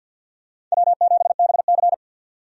Morse code, heard as a keyed tone, about 52 wpm, 700 Hz. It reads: W7BC
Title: USA Call Signs - 4 Characters - 50wpm